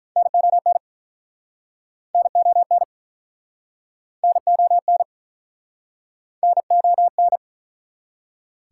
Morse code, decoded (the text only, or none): NON NON NON NON